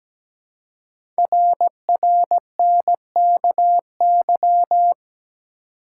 Morse code, heard as a keyed tone, about 17 wpm, 700 Hz. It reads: RRNKY
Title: Code Groups - 5 Letters - 17wpm